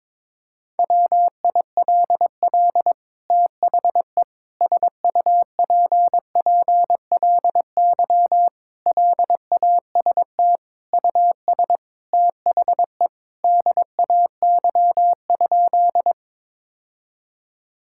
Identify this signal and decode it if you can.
Morse code, heard as a keyed tone, about 22 wpm, 700 Hz. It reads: WILL THE SUPPLY LAST US THE DAY?
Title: Sentences from Top 1000 Words - Review Part 5 - 22wpm